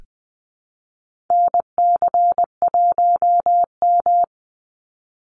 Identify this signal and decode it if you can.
Morse code, heard as a keyed tone, about 20 wpm, 700 Hz. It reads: NC1M
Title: Call Signs Repeated 2x - 20wpm